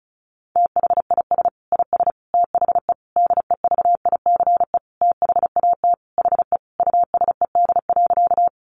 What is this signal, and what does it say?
Morse code, heard as a keyed tone, about 34 wpm, 700 Hz. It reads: THIS IS THE DEVICE THAT HE USED.
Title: Sentences from Top 1500 Words 35wpm